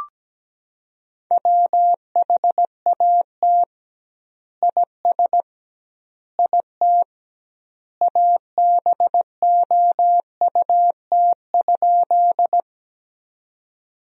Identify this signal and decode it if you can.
Morse code, heard as a keyed tone, about 17 wpm, 700 Hz. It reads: WHAT IS IT ABOUT?
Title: 2x Word-Spacing - Sentences from Top 100 Words - 17wpm